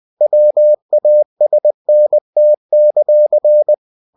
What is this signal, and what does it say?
Morse code, heard as a keyed tone, about 20 wpm, 600 Hz. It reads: WASNT;
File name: cw-books / winnie-the-pooh / winnie-20wpm-page-0071